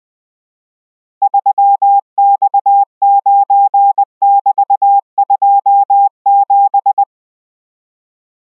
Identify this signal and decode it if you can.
Morse code, heard as a keyed tone, about 20 wpm, 800 Hz. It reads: 3X9<BT>27